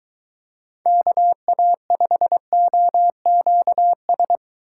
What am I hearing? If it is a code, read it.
Morse code, heard as a keyed tone, about 23 wpm, 700 Hz. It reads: KA5OQS